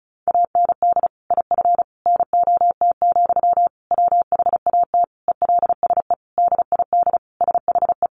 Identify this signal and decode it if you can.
Morse code, heard as a keyed tone, about 34 wpm, 700 Hz. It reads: AND IF NOT, WHAT ELSE DID SHE
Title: Sentences from Top 1400 Words 35wpm